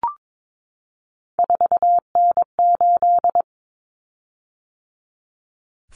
Morse code, heard as a keyed tone, about 22 wpm, 700 Hz. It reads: 4N8
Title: Common Cut Numbers - 3-Digit 22wpm